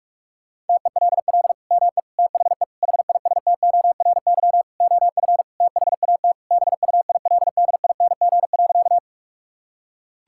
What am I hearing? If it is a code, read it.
Morse code, heard as a keyed tone, about 46 wpm, 700 Hz. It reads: TELL ME THE HISTORY OF THAT BUILDING.
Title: Sentences from Top 1000 Words - Review Part 5 - 45wpm